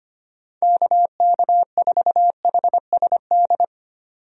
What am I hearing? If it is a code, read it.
Morse code, heard as a keyed tone, about 25 wpm, 700 Hz. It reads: KK4HSD
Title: USA Call Signs - 6 Characters - 25wpm